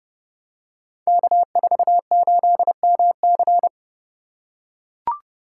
Morse code, heard as a keyed tone, about 29 wpm, 700 Hz. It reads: K48MC
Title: Code Groups - 5 Letter:Numbers - 30wpm